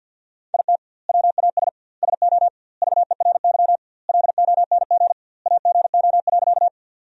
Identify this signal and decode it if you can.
Morse code, heard as a keyed tone, about 50 wpm, 700 Hz. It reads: IT WAS SO VERY LONG AGO.
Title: 2x Word-Spacing - Sentences from Top 500 Words - 50wpm